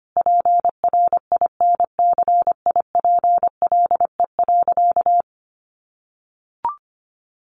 Morse code, heard as a keyed tone, about 25 wpm, 700 Hz. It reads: PRINCIPLE.
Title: Sentences from Top 1400 Words 25wpm